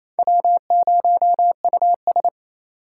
Morse code, heard as a keyed tone, about 37 wpm, 700 Hz. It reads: W0US